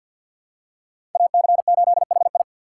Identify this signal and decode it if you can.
Morse code, heard as a keyed tone, about 47 wpm, 700 Hz. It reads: AC8HI
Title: USA Call Signs - 5 Characters - 6x Character Spacing - 50wpm